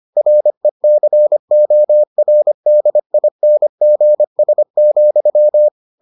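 Morse code, heard as a keyed tone, about 25 wpm, 600 Hz. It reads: RECORDINGS,
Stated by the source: Morse edition of the Foundations of Amateur Radio episodes.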